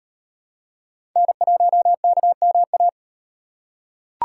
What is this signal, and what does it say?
Morse code, heard as a keyed tone, about 38 wpm, 700 Hz. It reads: N1KMA